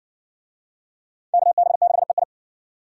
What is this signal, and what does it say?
Morse code, heard as a keyed tone, about 53 wpm, 700 Hz. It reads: KB6EI